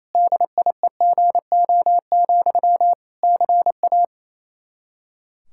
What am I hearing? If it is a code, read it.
Morse code, heard as a keyed tone, about 28 wpm, 700 Hz. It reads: DIEGO, CA
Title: Practice QSO - Exchange 1 - Beginning - 28wpm